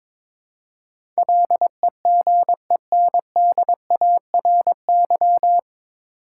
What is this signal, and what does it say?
Morse code, heard as a keyed tone, about 22 wpm, 700 Hz. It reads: LEGENDARY
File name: 9 Letter Words - First 3 Letters Given - 22wpm